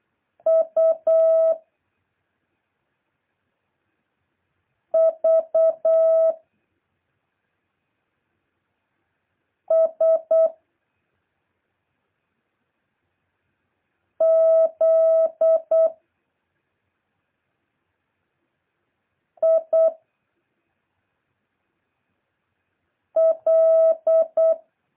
Morse code, heard as a keyed tone, about 8 wpm, 650 Hz. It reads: U V S Z I L
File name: jh8cbh / 2023cw